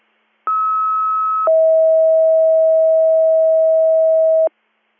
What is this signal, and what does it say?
Two-tone page, heard as a one-second tone, then a three-second tone, about 1290 then 640 Hz. It is heard from 0.5 to 4.5 s.
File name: 100- .0-call 3224